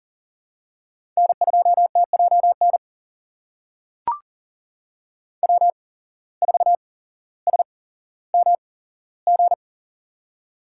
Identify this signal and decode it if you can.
Morse code, heard as a keyed tone, about 40 wpm, spaced as for 17 wpm, 700 Hz. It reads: N2TJN W4SMG
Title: USA Call Signs - 5 Characters - 8x Character Spacing - 40wpm